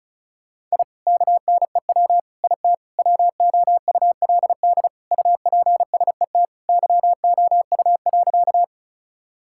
Morse code, heard as a keyed tone, about 36 wpm, 700 Hz. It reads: I KNEW IT WOULD UPSET YOU.
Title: Sentences from Top 1000 Words - Review Part 5 - 35wpm